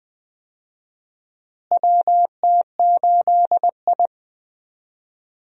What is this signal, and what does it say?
Morse code, heard as a keyed tone, about 20 wpm, 700 Hz. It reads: WT8I
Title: USA Call Signs - 4 Characters - Repeated 2x - 20wpm